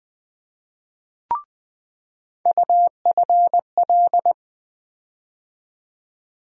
Morse code, heard as a keyed tone, about 20 wpm, 700 Hz. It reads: UFL
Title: USA Call Sign Postfixes - 3 Letters 20wpm